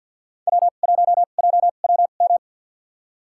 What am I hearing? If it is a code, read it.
Morse code, heard as a keyed tone, about 52 wpm, spaced as for 37 wpm, 700 Hz. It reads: W1JWM
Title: USA Call Signs - 2x Character Spacing - 50wpm